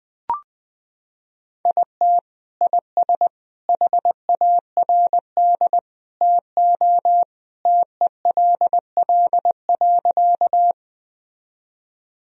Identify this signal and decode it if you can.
Morse code, heard as a keyed tone, about 20 wpm, 700 Hz. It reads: IT IS HARD TO TELL.
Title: Sentences from Top 1000 Words - Review Part 1 - 20wpm